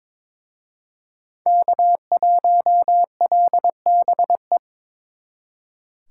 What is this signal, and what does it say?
Morse code, heard as a keyed tone, about 26 wpm, 700 Hz. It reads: K1LBE
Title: Speed Racing - USA Call Signs 22wpm